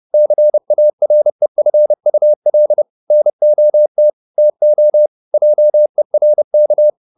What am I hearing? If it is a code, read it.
Morse code, heard as a keyed tone, about 30 wpm, 600 Hz. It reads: CAREFUL NOT TO JERK
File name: the old man and the sea / the old man and the sea 1 30wpm